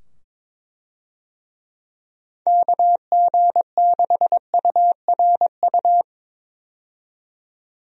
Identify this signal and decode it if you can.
Morse code, heard as a keyed tone, about 21 wpm, 700 Hz. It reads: KG6URU
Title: USA Call Signs - 6 Characters - 22wpm